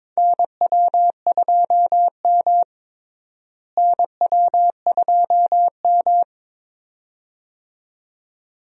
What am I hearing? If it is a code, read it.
Morse code, heard as a keyed tone, about 22 wpm, 700 Hz. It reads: NW2M NW2M
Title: USA Call Signs - 4 Characters - Repeated 2x - 22wpm